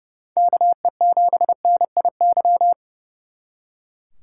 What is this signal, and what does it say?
Morse code, heard as a keyed tone, about 39 wpm, 700 Hz. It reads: KE7NIY